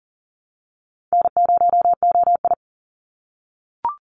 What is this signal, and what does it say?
Morse code, heard as a keyed tone, about 41 wpm, 700 Hz. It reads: N0OI
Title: NA SOTA Activator Callsigns 40wpm